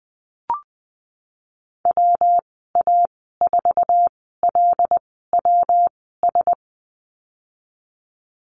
Morse code, heard as a keyed tone, about 19 wpm, spaced as for 14 wpm, 700 Hz. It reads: WA4LWS
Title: 128 Lesson 21g - Number 6 - Call Signs 20wpm